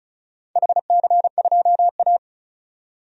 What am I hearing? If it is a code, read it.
Morse code, heard as a keyed tone, about 36 wpm, 700 Hz. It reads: HC2A
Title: DXCC-100 Call Signs - 4 Characters - 35wpm